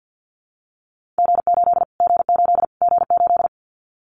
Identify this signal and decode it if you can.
Morse code, heard as a keyed tone, about 50 wpm, 700 Hz. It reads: G8 G8 G8
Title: Callsign Prefixes for Top 25 DXCC - Repeated 3x - 50wpm